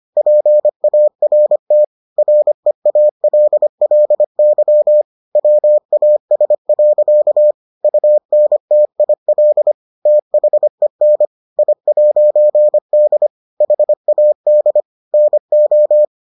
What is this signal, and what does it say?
Morse code, heard as a keyed tone, about 25 wpm, 600 Hz. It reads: PART REALLY WAS. UNTIL THEN I'D HAD NO